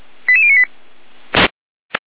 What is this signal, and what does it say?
Five-tone selective call (ZVEI), heard as quick runs of five tones, tones of about 70 ms. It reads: 88098